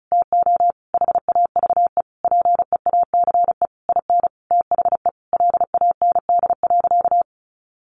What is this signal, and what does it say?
Morse code, heard as a keyed tone, about 34 wpm, 700 Hz. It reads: TO HAVE PEACE IN THE LAND.
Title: Sentences from Top 1500 Words 35wpm